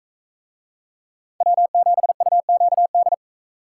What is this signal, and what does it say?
Morse code, heard as a keyed tone, about 44 wpm, 700 Hz. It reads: W7UQD